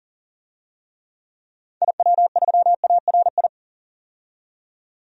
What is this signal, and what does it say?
Morse code, heard as a keyed tone, about 40 wpm, 700 Hz. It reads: IW3ARI
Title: DXCC-100 Call Signs - 6 Characters - 40wpm